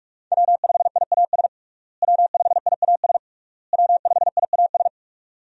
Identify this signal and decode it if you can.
Morse code, heard as a keyed tone, about 46 wpm, 700 Hz. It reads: W5IAS W5IAS W5IAS
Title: USA Call Signs - 5 Characters - Repeated 3x - 45wpm